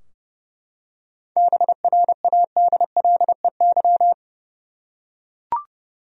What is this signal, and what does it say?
Morse code, heard as a keyed tone, about 32 wpm, 700 Hz. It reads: BRADLEY
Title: USA Common Names - 30wpm